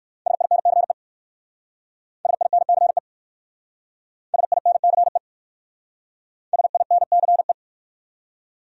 Morse code, heard as a keyed tone, about 60 wpm, 700 Hz. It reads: SINCE SINCE SINCE SINCE